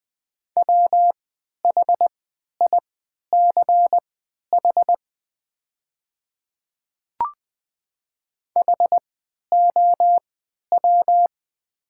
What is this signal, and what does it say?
Morse code, heard as a keyed tone, about 20 wpm, spaced as for 11 wpm, 690 Hz. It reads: WHICH HOW
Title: 3x Character Spacing - Top 200 Words - 20wpm